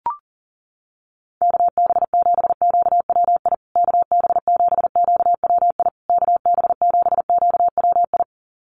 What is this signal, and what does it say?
Morse code, heard as a keyed tone, about 40 wpm, 700 Hz. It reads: KB7QWI KB7QWI KB7QWI
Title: USA Call Signs Repeated 3x 40wpm